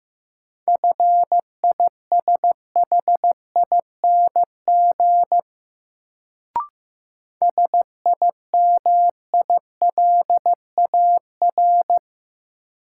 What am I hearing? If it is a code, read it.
Morse code, heard as a keyed tone, about 15 wpm, 700 Hz. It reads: FISHING SIMILAR
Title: Sentences from Top 1000 Words - Review Part 5 - New Word Drill - 15wpm